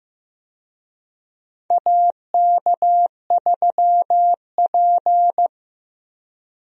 Morse code, heard as a keyed tone, about 15 wpm, 700 Hz. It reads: AK3P
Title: USA Call Signs - 4 Characters - Repeated 2x - 15wpm